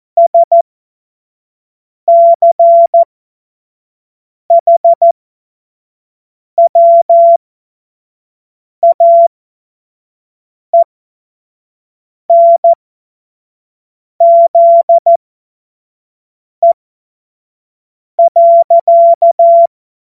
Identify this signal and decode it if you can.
Morse code, heard as a keyed tone, about 14 wpm, spaced as for 5 wpm, 680 Hz. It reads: SCHWAENZE.
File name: Morsekurs / Witze / Witz 05. April 2022 25 F70